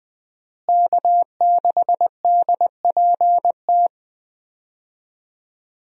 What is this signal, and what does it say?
Morse code, heard as a keyed tone, about 20 wpm, 700 Hz. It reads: K6DPT
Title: USA Call Signs - 5 Characters - Repeated 2x - 20wpm